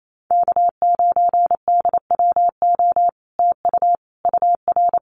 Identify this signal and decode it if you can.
Morse code, heard as a keyed tone, about 28 wpm, 700 Hz. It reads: K9DWO TU UR
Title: Common QSO Phrases Encore 28wpm